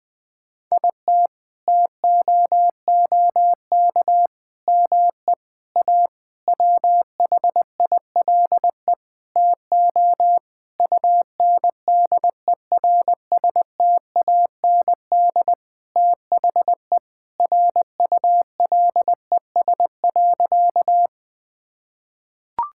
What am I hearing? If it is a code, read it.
Morse code, heard as a keyed tone, about 20 wpm, 700 Hz. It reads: IT TOOK ME A WHILE TO UNDERSTAND THE RULES.
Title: Sentences with 6-Word Phrases - Spoken Beginning - 20wpm